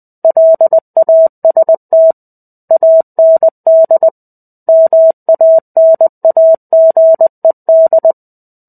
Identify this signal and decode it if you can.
Morse code, heard as a keyed tone, about 20 wpm, 650 Hz. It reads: LAST AND MANAGED
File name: 05-Chapter-05 20 WPM